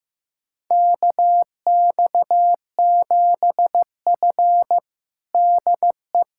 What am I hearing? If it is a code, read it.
Morse code, heard as a keyed tone, about 16 wpm, 700 Hz. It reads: KX7F DE
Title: USA Callsign DE Callsign - 4 Characters - 15wpm